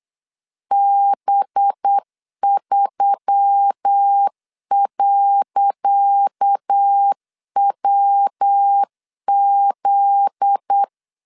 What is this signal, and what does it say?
Morse code, heard as a keyed tone, about 8 wpm, 800 Hz. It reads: B3.WZ
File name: LEZIONI / 10 SIGNS / 10C DOT 8wpm 3649